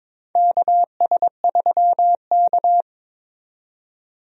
Morse code, heard as a keyed tone, about 22 wpm, 700 Hz. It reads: KS3K
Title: USA Call Signs - 4 Characters - 22wpm